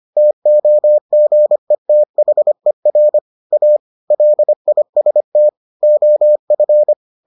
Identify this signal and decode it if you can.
Morse code, heard as a keyed tone, about 25 wpm, 600 Hz. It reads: TOGETHER A LIST OF